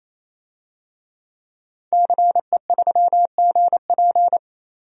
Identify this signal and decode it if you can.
Morse code, heard as a keyed tone, about 27 wpm, 700 Hz. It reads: CE3GP
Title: DXCC-100 Call Signs - 5 Characters - 28wpm